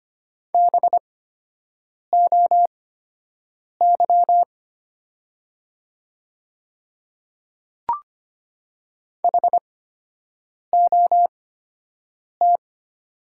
Morse code, heard as a keyed tone, about 25 wpm, spaced as for 7 wpm, 700 Hz. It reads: BOY HOT
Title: 8x Character Spacing - Top 200 Words - 25wpm